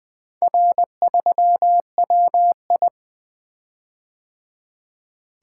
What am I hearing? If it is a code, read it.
Morse code, heard as a keyed tone, about 24 wpm, 700 Hz. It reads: R3WI
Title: Speed-Racing - Four Letter-Number Combinations - 20wpm